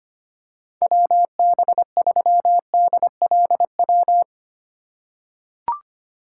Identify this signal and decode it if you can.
Morse code, heard as a keyed tone, about 25 wpm, 700 Hz. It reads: WB3DLW
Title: USA Call Signs - 6 Characters - 3x Character Spacing - 25wpm